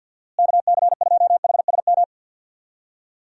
Morse code, heard as a keyed tone, about 48 wpm, 700 Hz. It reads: KC2HSM
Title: USA Call Signs - 6 Characters - Repeated 2x - 50wpm